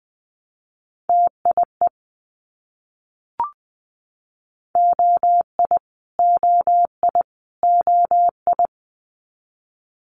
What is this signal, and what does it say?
Morse code, heard as a keyed tone, about 20 wpm, 700 Hz. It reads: TIE OI OI OI
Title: Multi-letter Phonemes Repeated 3x 20wpm